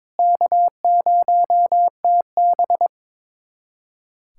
Morse code, heard as a keyed tone, about 22 wpm, 700 Hz. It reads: K0TB
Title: USA Callsign DE Callsign - 4 Characters - 22wpm